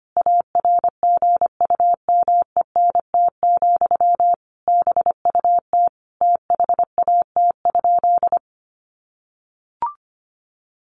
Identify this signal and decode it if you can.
Morse code, heard as a keyed tone, about 25 wpm, 700 Hz. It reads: ARGUMENT, BUT THAT?
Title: Sentences from Top 1400 Words 25wpm